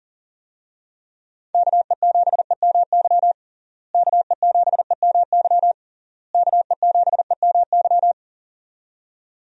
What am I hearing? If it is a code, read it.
Morse code, heard as a keyed tone, about 40 wpm, 700 Hz. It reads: KE7EMY KE7EMY KE7EMY
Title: USA Call Signs - 6 Characters - Repeated 3x - 40wpm